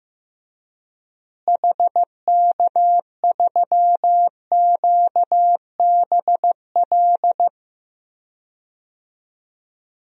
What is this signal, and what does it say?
Morse code, heard as a keyed tone, about 15 wpm, 700 Hz. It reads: HK3QBL
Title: DXCC-100 Call Signs - 6 Characters - Repeated 2x - 15wpm